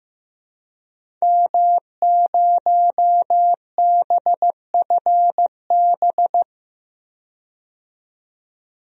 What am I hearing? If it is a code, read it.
Morse code, heard as a keyed tone, about 15 wpm, 700 Hz. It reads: M0BFB
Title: DXCC-100 Call Signs - 5 Characters - 15wpm